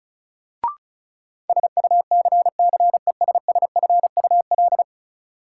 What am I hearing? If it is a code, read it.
Morse code, heard as a keyed tone, about 34 wpm, 700 Hz. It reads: SUCCESSFUL
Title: Sentences from Top 900 Words - New Word Drill - 35wpm